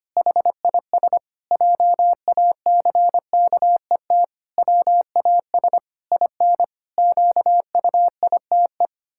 Morse code, heard as a keyed tone, about 25 wpm, 700 Hz. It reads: HIS JACKET WAS IN QUITE
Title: Pangram Challenge v2 - 25wpm